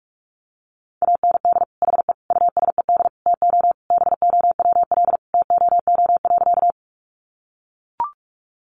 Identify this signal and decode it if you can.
Morse code, heard as a keyed tone, about 44 wpm, 700 Hz. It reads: AND HE USED TO BOWL TOO.
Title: Sentences from Top 1200 Words 45wpm